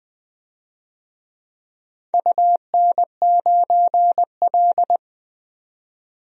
Morse code, heard as a keyed tone, about 20 wpm, 700 Hz. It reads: UN9L
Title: DXCC-100 Call Signs - 4 Characters - 20wpm